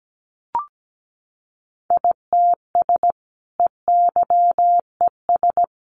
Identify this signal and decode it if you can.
Morse code, heard as a keyed tone, about 17 wpm, 700 Hz. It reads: ITS EYES
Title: A Sound Of Thunder 17wpm